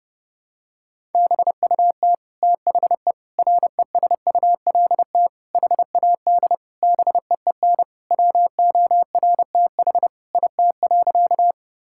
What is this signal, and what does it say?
Morse code, heard as a keyed tone, about 32 wpm, 700 Hz. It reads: BUT THE RESULT HAD BEEN WORTH IT.
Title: Sentences from Top 1000 Words - Review Part 5 - 30wpm